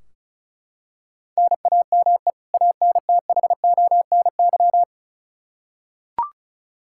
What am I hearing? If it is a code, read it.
Morse code, heard as a keyed tone, about 36 wpm, 700 Hz. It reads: NAME ANTHONY
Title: K1USN SST QSO Practice 35wpm